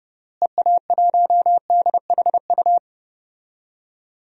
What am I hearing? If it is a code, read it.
Morse code, heard as a keyed tone, about 31 wpm, 700 Hz. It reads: EA1DHU
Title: DXCC-100 Call Signs - 6 Characters - Repeated 3x - 30wpm